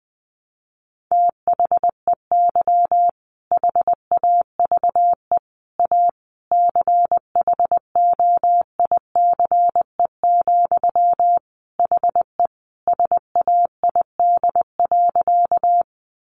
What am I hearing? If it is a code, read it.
Morse code, heard as a keyed tone, about 20 wpm, 700 Hz. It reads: THEY HAVE A CHOICE, HE SAID.
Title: Sentences from Top 900 Words 20wpm